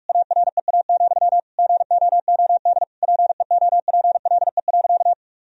Morse code, heard as a keyed tone, about 46 wpm, 700 Hz. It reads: AREA, GOOD PEOPLE.
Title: Sentences from Top 1000 Words - Review Part 2 - 45wpm